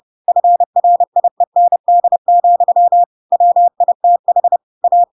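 Morse code, heard as a keyed tone, about 30 wpm, 700 Hz. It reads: FRIEND, WITH A
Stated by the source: Chapter 2 of The Adventures of Pinocchio read as a book in Morse code (CW) at 30 wpm, by ik1hge.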